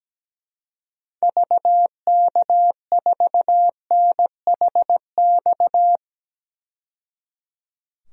Morse code, heard as a keyed tone, about 17 wpm, 700 Hz. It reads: VK4NHX
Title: DXCC-100 Call Signs - 6 Characters - 17wpm